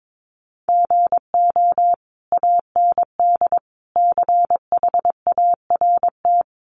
Morse code, heard as a keyed tone, about 22 wpm, 700 Hz. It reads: GO AND CHART
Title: Sentences from Top 1300 Words 22wpm